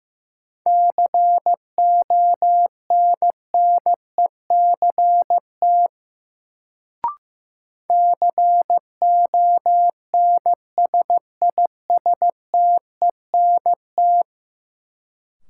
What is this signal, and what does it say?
Morse code, heard as a keyed tone, about 15 wpm, 700 Hz. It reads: CONNECT CONSISTENT
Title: Sentences from Top 1500 Words - New Word Drill - 15wpm